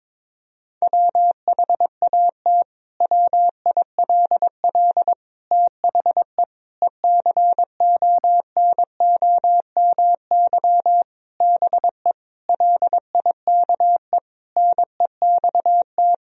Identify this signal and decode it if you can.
Morse code, heard as a keyed tone, about 22 wpm, 700 Hz. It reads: WHAT WILL THE ECONOMY BE LIKE NEXT
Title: Sentences from Top 1000 Words - Review Part 3 - 22wpm